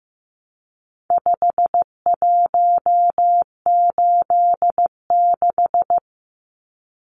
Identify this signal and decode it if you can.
Morse code, heard as a keyed tone, about 15 wpm, 700 Hz. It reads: 5186
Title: Numbers - 4-Digit 15wpm